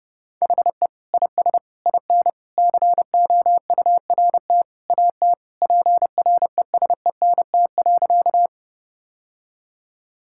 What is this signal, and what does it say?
Morse code, heard as a keyed tone, about 30 wpm, 700 Hz. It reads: HE IS IN COURT AT PRESENT.
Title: Sentences from Top 1600 Words 30wpm